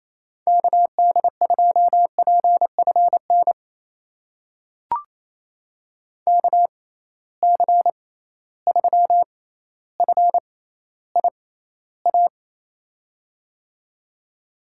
Morse code, heard as a keyed tone, about 28 wpm, spaced as for 14 wpm, 700 Hz. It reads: KD2PFN KC3FIA
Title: USA Call Signs - 6 Characters - 6x Character Spacing - 28wpm